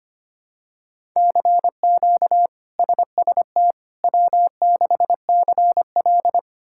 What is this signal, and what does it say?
Morse code, heard as a keyed tone, about 25 wpm, 700 Hz. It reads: CQ SST W6CL
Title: K1USN SST QSO Practice - 25wpm